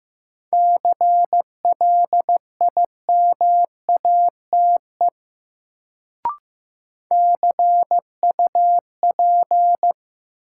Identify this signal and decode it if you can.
Morse code, heard as a keyed tone, about 15 wpm, 700 Hz. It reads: CLIMATE CUP
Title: Sentences from Top 1600 Words - New Word Drill - 15wpm